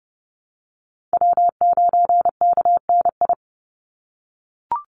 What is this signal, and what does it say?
Morse code, heard as a keyed tone, about 30 wpm, 700 Hz. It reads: W9KNI
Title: Call Signs - 30wpm